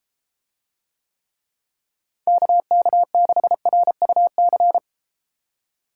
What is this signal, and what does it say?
Morse code, heard as a keyed tone, about 32 wpm, 700 Hz. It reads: KK6RUC